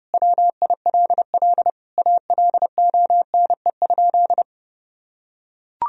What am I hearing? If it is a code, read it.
Morse code, heard as a keyed tone, about 32 wpm, 700 Hz. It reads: WILL ALONE?
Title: Sentences from Top 1000 Words - Review Part 2 - 30wpm